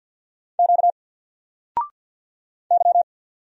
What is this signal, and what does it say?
Morse code, heard as a keyed tone, about 48 wpm, 700 Hz. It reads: Q Y
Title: Single Letter-Number Alt 50wpm